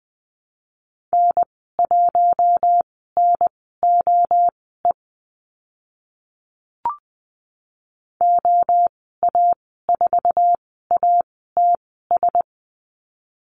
Morse code, heard as a keyed tone, about 20 wpm, spaced as for 14 wpm, 700 Hz. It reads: N1NOE OA4ATS
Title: 069 Lesson 12g - Letter d - Call Signs 20wpm